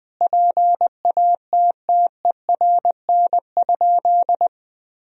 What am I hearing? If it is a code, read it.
Morse code, heard as a keyed tone, about 20 wpm, 700 Hz. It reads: PATTERN?
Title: Sentences from Top 1000 Words - Review Part 2 - 20wpm